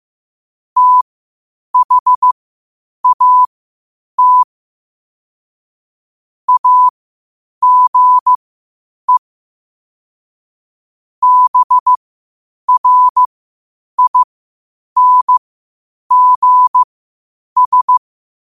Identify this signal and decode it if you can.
Morse code, heard as a keyed tone, about 15 wpm, spaced as for 8 wpm, 1000 Hz. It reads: THAT AGE BRINGS
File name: qotdfiles / 8wpm